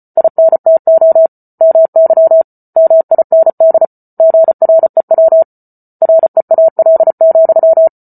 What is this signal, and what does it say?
Morse code, heard as a keyed tone, about 34 wpm, 650 Hz. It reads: INTO MY MIND GREW REAL,
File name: Book Two / 08-Chapter 8 - 34 WPM